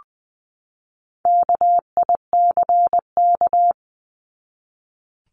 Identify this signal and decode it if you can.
Morse code, heard as a keyed tone, about 20 wpm, 700 Hz. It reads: KICK
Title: Top 1251-1500 Words 20wpm